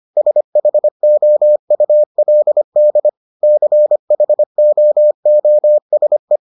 Morse code, heard as a keyed tone, about 25 wpm, 600 Hz. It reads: SHOULD CHOOSE